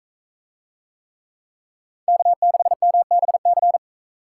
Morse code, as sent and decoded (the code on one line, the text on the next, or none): -.- -.... -- -... -.-.
K6MBC